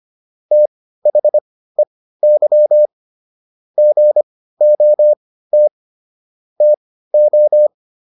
Morse code, heard as a keyed tone, about 25 wpm, spaced as for 15 wpm, 600 Hz. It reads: THEY GOT TO